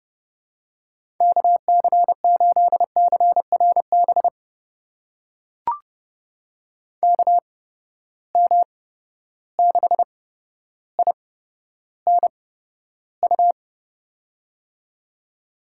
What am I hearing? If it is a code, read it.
Morse code, heard as a keyed tone, about 32 wpm, spaced as for 13 wpm, 700 Hz. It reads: KC8CRB KM6INU
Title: USA Call Signs - 8x Character Spacing - 30wpm